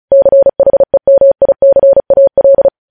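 Morse code, heard as a keyed tone, about 36 wpm, 550 Hz. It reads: CHEMICAL
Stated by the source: News headlines sent in Morse at 35 WPM.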